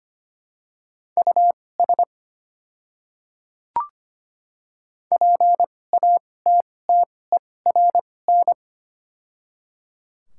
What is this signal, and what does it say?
Morse code, heard as a keyed tone, about 25 wpm, spaced as for 18 wpm, 700 Hz. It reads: US PATTERN
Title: 2x Character Spacing - Top 500 Words - 25wpm